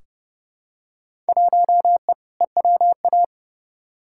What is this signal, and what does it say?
Morse code, heard as a keyed tone, about 30 wpm, 700 Hz. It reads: 1E EWA
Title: Field Day Exchanges - 30wpm